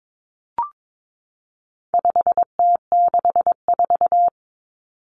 Morse code, heard as a keyed tone, about 22 wpm, 700 Hz. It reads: 5T64
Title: Common Cut Numbers - 4-Digit 22wpm